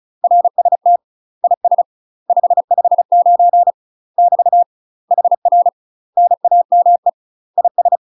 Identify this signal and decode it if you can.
Morse code, heard as a keyed tone, about 36 wpm, 700 Hz. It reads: RST IS 559 <BT> HR NAME IS
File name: UK MASTER QSOs TRAINING / 35WPM / UK MASTER QSOs 0038